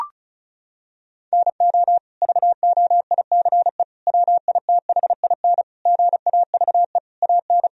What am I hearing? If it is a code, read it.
Morse code, heard as a keyed tone, about 34 wpm, 700 Hz. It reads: NO VOICE WITHIN GAVE AN
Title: Sentences from Top 1000 Words - Review Part 4 - 35wpm